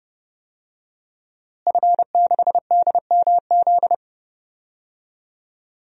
Morse code, heard as a keyed tone, about 32 wpm, 700 Hz. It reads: F6DMZ
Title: DXCC-100 Call Signs - 5 Characters - Repeated 2x - 30wpm